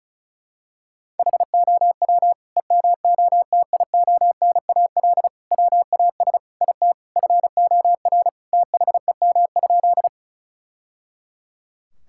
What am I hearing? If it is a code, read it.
Morse code, heard as a keyed tone, about 34 wpm, 700 Hz. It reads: HOW EMOTIONAL WAS IT FOR THEM?
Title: Sentences from Top 1000 Words - Review Part 5 - 35wpm